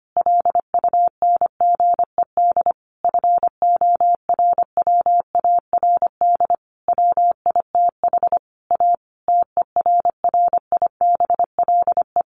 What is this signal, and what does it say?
Morse code, heard as a keyed tone, about 25 wpm, 700 Hz. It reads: LUNGED FORWARD WITH A TERRIBLE
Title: A Sound Of Thunder 25wpm